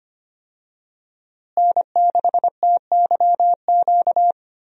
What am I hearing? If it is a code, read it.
Morse code, heard as a keyed tone, about 25 wpm, 700 Hz. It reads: N6TYQ
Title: USA Call Signs - 5 Characters - Repeated 2x - 25wpm